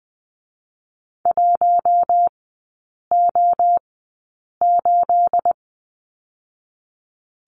Morse code, heard as a keyed tone, about 20 wpm, 700 Hz. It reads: 1 O 8
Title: 212 Lesson 34e - Number 8 - Three Characters 20wpm